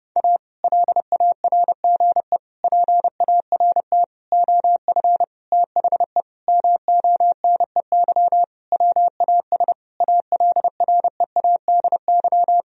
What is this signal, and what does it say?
Morse code, heard as a keyed tone, about 30 wpm, 700 Hz. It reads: A LARGE PART OF THE MONEY WAS ALREADY
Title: Sentences from Top 1000 Words - Review Part 1 - 30wpm